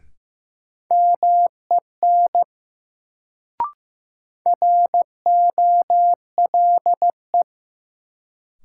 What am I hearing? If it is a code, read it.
Morse code, heard as a keyed tone, about 15 wpm, 700 Hz. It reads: MEN ROLE
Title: Sets of 3 Words - New Word Drill - 15wpm